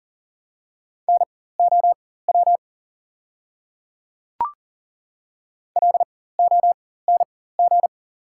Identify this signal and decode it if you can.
Morse code, heard as a keyed tone, about 40 wpm, spaced as for 19 wpm, 690 Hz. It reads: NOW LONG
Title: 4x Character Spacing - Top 100 Words - 40wpm